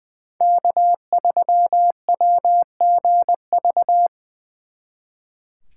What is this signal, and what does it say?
Morse code, heard as a keyed tone, about 24 wpm, 700 Hz. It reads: K3WGV
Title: Speed Racing - USA 1x3 Call Signs 20wpm